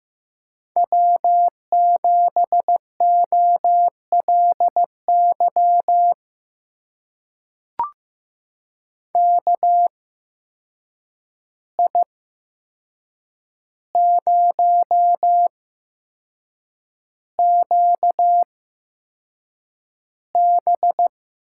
Morse code, heard as a keyed tone, about 15 wpm, spaced as for 6 wpm, 700 Hz. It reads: W7OLY KI0QB
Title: USA Call Signs - 8x Character Spacing - 15wpm